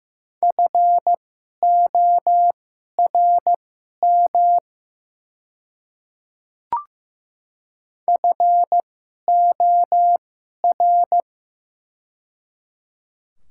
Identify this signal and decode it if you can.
Morse code, heard as a keyed tone, about 15 wpm, spaced as for 11 wpm, 700 Hz. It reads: FORM FOR
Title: 2x Character Spacing - Top 200 Words - 15wpm